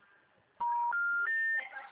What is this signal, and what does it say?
Special information tone, with noise.